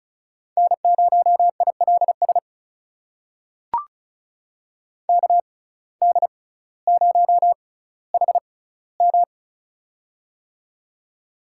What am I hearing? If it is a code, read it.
Morse code, heard as a keyed tone, about 36 wpm, spaced as for 18 wpm, 700 Hz. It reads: N0ILS KD0HM
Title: USA Call Signs - 5 Characters - 6x Character Spacing - 35wpm